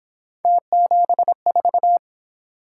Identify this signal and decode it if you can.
Morse code, heard as a keyed tone, about 26 wpm, 700 Hz. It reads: T74
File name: Speed-Racing - Three Letter-Number Combinations - 22wpm